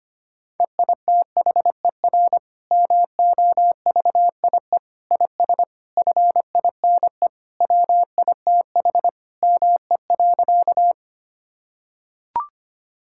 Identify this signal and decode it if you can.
Morse code, heard as a keyed tone, about 25 wpm, 700 Hz. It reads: EITHER MOVIE IS FINE WITH ME.
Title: Sentences from Top 1000 Words - Review Part 5 - 25wpm